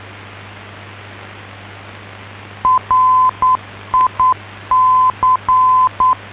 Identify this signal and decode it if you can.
Morse code, heard as a keyed tone, about 9 wpm, 1020 Hz. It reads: RIC